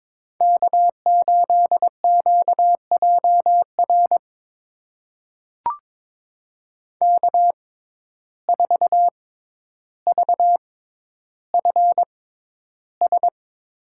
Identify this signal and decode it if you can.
Morse code, heard as a keyed tone, about 22 wpm, spaced as for 11 wpm, 700 Hz. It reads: K8QJR K4VFS
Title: USA Call Signs - 5 Characters - 6x Character Spacing - 22wpm